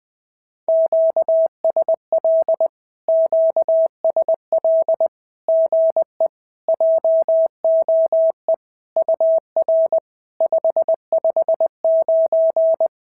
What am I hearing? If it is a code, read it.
Morse code, heard as a keyed tone, about 20 wpm, 690 Hz. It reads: QSL QSL GE JOE UR 559
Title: SOTA QSO Practice 20wpm